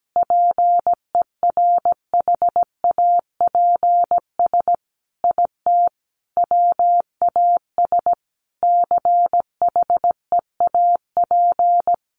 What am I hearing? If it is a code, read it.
Morse code, heard as a keyed tone, about 17 wpm, 700 Hz. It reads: PERHAPS IT WAS CHEAP
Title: Sentences from Top 1200 Words 17wpm